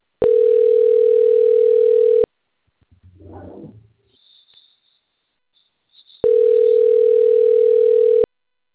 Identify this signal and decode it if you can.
Ringback tone (North America), 440 + 480 Hz.